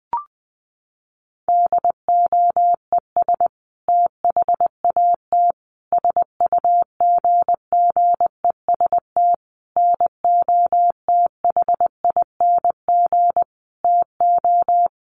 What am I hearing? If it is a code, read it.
Morse code, heard as a keyed tone, about 20 wpm, 700 Hz. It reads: DOES THAT SUGGEST NOTHING TO
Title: Sentences from Top 1300 Words 20wpm